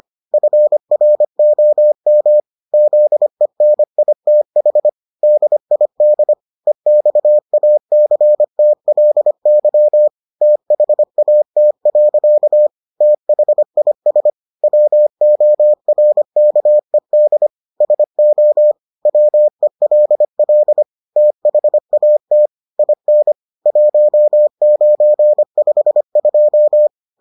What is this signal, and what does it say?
Morse code, heard as a keyed tone, about 25 wpm, 600 Hz. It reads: FROM ZENITH DID EXACTLY THAT. THIS WORKED SO WELL THAT IN 1952